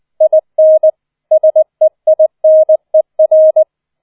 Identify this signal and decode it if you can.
Morse code, heard as a keyed tone, about 20 wpm, 630 Hz. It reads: IN SEINER